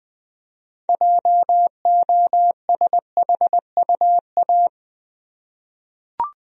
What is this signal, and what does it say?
Morse code, heard as a keyed tone, about 20 wpm, 700 Hz. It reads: JOSHUA
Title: USA Common Names - 20wpm